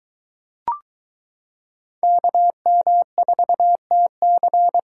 Morse code, heard as a keyed tone, about 23 wpm, 700 Hz. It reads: KM4TC